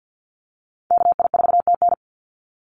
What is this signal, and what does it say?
Morse code, heard as a keyed tone, about 50 wpm, 700 Hz. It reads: KI4TN
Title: NA SOTA Activator Callsigns 50wpm